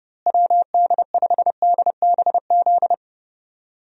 Morse code, heard as a keyed tone, about 30 wpm, 700 Hz. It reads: WD5DBZ
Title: USA Call Signs - 6 Characters - Repeated 2x - 30wpm